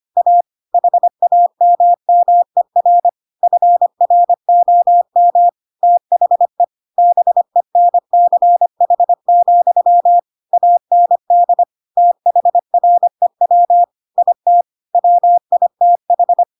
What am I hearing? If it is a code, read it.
Morse code, heard as a keyed tone, about 25 wpm, 700 Hz. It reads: A HAMMER FROM THE BENCH, AND THREW IT WITH
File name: Pinocchio / Pinocchio ch4 25 25 ik1hge